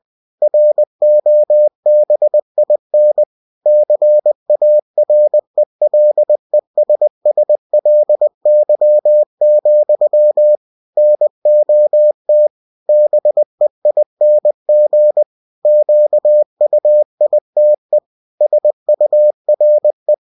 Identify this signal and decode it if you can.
Morse code, heard as a keyed tone, about 20 wpm, 600 Hz. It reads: ROBIN CARELESSLY, NOT BEING QUITE SURE